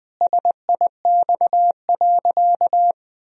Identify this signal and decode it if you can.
Morse code, heard as a keyed tone, about 20 wpm, 700 Hz. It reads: SIX.
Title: Sentences from Top 1000 Words - Review Part 2 - 20wpm